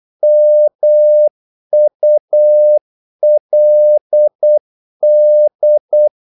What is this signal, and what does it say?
Morse code, heard as a keyed tone, about 8 wpm, 600 Hz. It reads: MULD
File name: Korte ord DK 8wpm